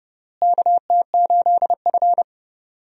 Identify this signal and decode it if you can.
Morse code, heard as a keyed tone, about 32 wpm, 700 Hz. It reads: KT8F
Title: USA Call Signs - 4 Characters - Repeated 3x - 30wpm